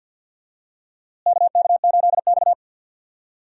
Morse code, heard as a keyed tone, about 51 wpm, 700 Hz. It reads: KK8X
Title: USA Call Signs - 4 Characters - Repeated 2x - 50wpm